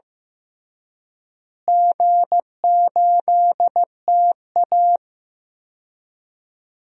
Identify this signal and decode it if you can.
Morse code, heard as a keyed tone, about 15 wpm, 700 Hz. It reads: G8TA
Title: DXCC-100 Call Signs - 4 Characters - Repeated 3x - 15wpm